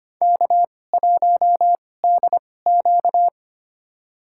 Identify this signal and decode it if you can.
Morse code, heard as a keyed tone, about 25 wpm, spaced as for 18 wpm, 700 Hz. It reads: K1DQ
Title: USA Call Signs - 2x Character Spacing - 25wpm